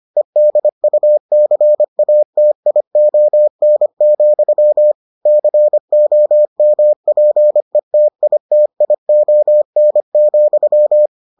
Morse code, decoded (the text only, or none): EDUCATION, COMPETITION,